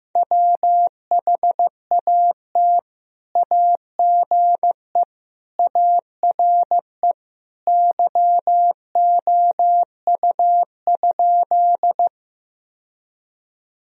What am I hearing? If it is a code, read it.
Morse code, heard as a keyed tone, about 15 wpm, 700 Hz. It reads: WHAT AGE ARE YOU?
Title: Sentences from Top 1000 Words - Review Part 3 - 15wpm